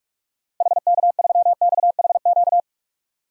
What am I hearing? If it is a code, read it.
Morse code, heard as a keyed tone, about 42 wpm, 700 Hz. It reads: HK3XHQ
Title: DXCC-100 Call Signs - 6 Characters - 45wpm